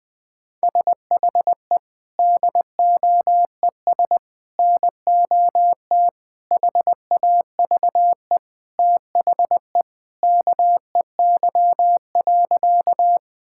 Morse code, heard as a keyed tone, about 20 wpm, 700 Hz. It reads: SHE DOES NOT HAVE THE KEY.
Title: Sentences with 3-Word Phrases - 20wpm